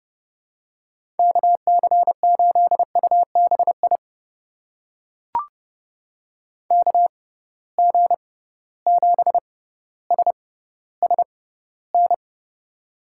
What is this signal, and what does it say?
Morse code, heard as a keyed tone, about 30 wpm, spaced as for 15 wpm, 700 Hz. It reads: KC8UBI KG7SSN
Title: USA Call Signs - 6 Characters - 6x Character Spacing - 30wpm